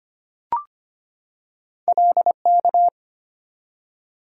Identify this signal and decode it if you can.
Morse code, heard as a keyed tone, about 25 wpm, 700 Hz. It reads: LK
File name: Two Letter-Number Combinations - 25wpm